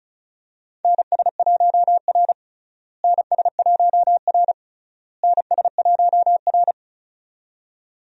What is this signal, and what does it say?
Morse code, heard as a keyed tone, about 36 wpm, 700 Hz. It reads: NS1R NS1R NS1R
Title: USA Call Signs - 4 Characters - Repeated 3x - 35wpm